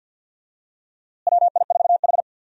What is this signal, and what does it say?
Morse code, heard as a keyed tone, about 53 wpm, 700 Hz. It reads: WI4H
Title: USA Call Signs - 4 Characters - Repeated 2x - 50wpm